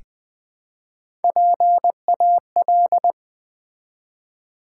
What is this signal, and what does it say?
Morse code, heard as a keyed tone, about 20 wpm, 700 Hz. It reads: PAL
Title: Build A Word - 1 Letter At A Time - 4-Letter Words - 20wpm